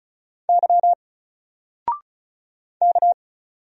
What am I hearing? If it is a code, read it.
Morse code, heard as a keyed tone, about 36 wpm, 700 Hz. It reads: Y K
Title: Single Character (Letter:Number:Punctuation:Prosign) 35wpm